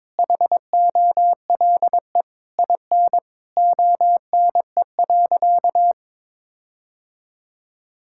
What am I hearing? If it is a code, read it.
Morse code, heard as a keyed tone, about 22 wpm, 700 Hz. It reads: HOLE IN ONE.
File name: All-Age Appropriate Jokes - 22wpm